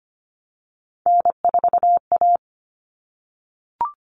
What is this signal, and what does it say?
Morse code, heard as a keyed tone, about 25 wpm, 700 Hz. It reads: N4A
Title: Common Cut Numbers - 3-Digit 25wpm